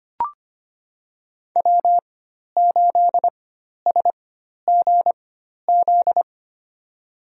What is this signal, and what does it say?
Morse code, heard as a keyed tone, about 25 wpm, spaced as for 12 wpm, 700 Hz. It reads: W8SGZ
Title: USA Call Signs - 5 Characters - 4x Character Spacing - 25wpm